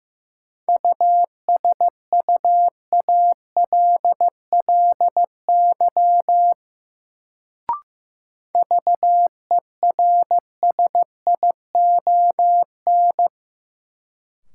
Morse code, heard as a keyed tone, about 15 wpm, 700 Hz. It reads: USUALLY VERSION
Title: Sentences from Top 1400 Words - New Word Drill Alt - 15wpm